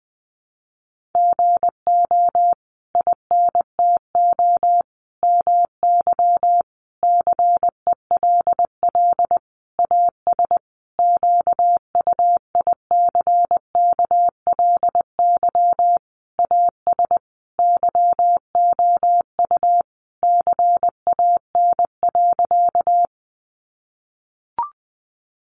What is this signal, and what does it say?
Morse code, heard as a keyed tone, about 20 wpm, 700 Hz. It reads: GO INTO MY CELL AS QUICKLY AS YOU CAN.
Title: Sentences from Top 1100 Words 20wpm